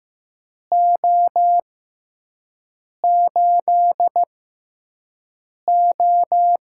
Morse code, heard as a keyed tone, about 15 wpm, spaced as for 5 wpm, 700 Hz. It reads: O8O
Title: USA Call Signs - 4 Characters - 6x Character Spacing - 15wpm